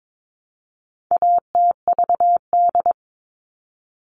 Morse code, heard as a keyed tone, about 22 wpm, 700 Hz. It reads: ATVD
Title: All Cut Numbers - 4-Digit 22wpm